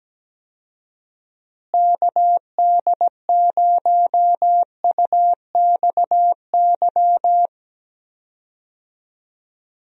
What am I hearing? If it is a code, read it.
Morse code, heard as a keyed tone, about 17 wpm, 700 Hz. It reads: KD0UXY
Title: USA Call Signs - 6 Characters - Repeated 3x - 17wpm